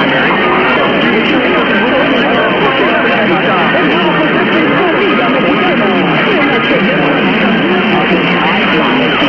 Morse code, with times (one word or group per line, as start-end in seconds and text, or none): none